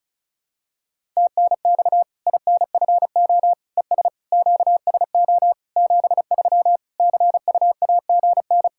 Morse code, heard as a keyed tone, about 36 wpm, 700 Hz. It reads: TNX INFO ES QSO 73 CUAGN
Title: K1USN SST QSO Practice 35wpm